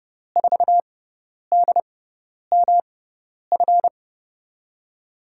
Morse code, heard as a keyed tone, about 30 wpm, spaced as for 10 wpm, 700 Hz. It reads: 4DMF
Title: USA Call Signs - 6x Character Spacing - 30wpm